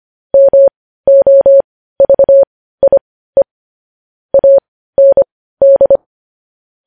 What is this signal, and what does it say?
Morse code, heard as a keyed tone, about 25 wpm, spaced as for 15 wpm, 550 Hz. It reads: MOVIE AND
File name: podcast / News headlines-1x-25x15